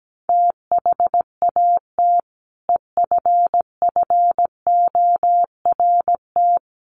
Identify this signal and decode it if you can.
Morse code, heard as a keyed tone, about 17 wpm, 700 Hz. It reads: THAT EFFORT
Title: Sentences from Top 1200 Words 17wpm